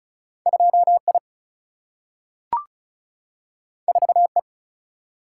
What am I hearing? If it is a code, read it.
Morse code, heard as a keyed tone, about 36 wpm, 700 Hz. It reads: 2I 4E
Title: Single-Letter Single-Number 35wpm